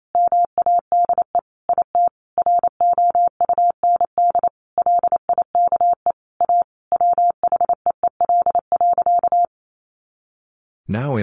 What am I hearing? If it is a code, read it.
Morse code, heard as a keyed tone, about 28 wpm, 700 Hz. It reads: MADE IT ROUND LIKE A WHEEL.
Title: Sentences from Top 500 Words 28wpm